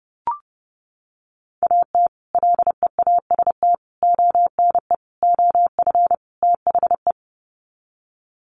Morse code, heard as a keyed tone, about 28 wpm, 700 Hz. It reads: AT LEAST ONE OF THE
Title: Sets of 5 Words Encore 30wpm